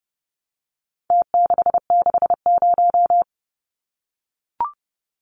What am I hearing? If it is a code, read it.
Morse code, heard as a keyed tone, about 30 wpm, 700 Hz. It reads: T660
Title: Numbers - Contest Style - 4-Digit 30wpm